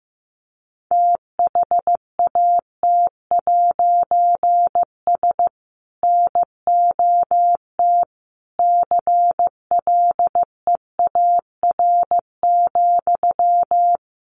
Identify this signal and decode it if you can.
Morse code, heard as a keyed tone, about 14 wpm, 700 Hz. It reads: THAT'S NOT CLEAR,
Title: A Sound Of Thunder 15wpm